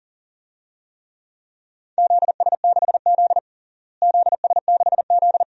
Morse code, heard as a keyed tone, about 40 wpm, 700 Hz. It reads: ZS6Z ZS6Z
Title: DXCC-100 Call Signs - 4 Characters - Repeated 2x - 40wpm